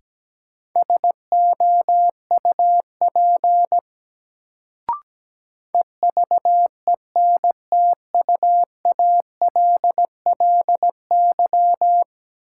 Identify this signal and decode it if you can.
Morse code, heard as a keyed tone, about 17 wpm, 700 Hz. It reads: SOUP EVENTUALLY
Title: Sentences from Top 1000 Words - Review Part 6 - New Word Drill - 17wpm